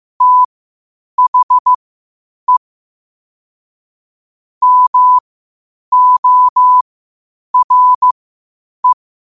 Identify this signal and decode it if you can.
Morse code, heard as a keyed tone, about 15 wpm, spaced as for 8 wpm, 1000 Hz. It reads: THE MORE
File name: qotdfiles / 8wpm